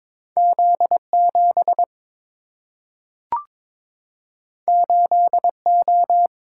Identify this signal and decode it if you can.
Morse code, heard as a keyed tone, about 22 wpm, 700 Hz. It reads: Z 7 8 O
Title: Single-Letter Single-Number 22wpm